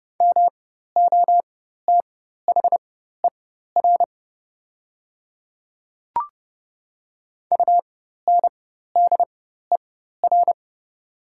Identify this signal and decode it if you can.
Morse code, heard as a keyed tone, about 30 wpm, spaced as for 14 wpm, 700 Hz. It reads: MOTHER UNDER
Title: 4x Character Spacing - Top 300 Words - 30wpm